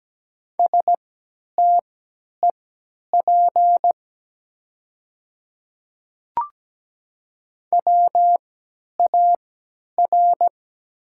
Morse code, heard as a keyed tone, about 17 wpm, spaced as for 10 wpm, 700 Hz. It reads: STEP WAR
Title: 3x Character Spacing - Top 500 Words - 17wpm